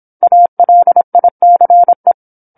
Morse code, heard as a keyed tone, about 27 wpm, 700 Hz. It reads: ALICE
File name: 08-Chapter-08 26 WPM